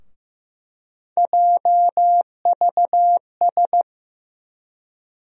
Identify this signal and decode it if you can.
Morse code, heard as a keyed tone, about 15 wpm, 700 Hz. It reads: JVS
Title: USA Call Sign Postfixes - 3 Letters 15wpm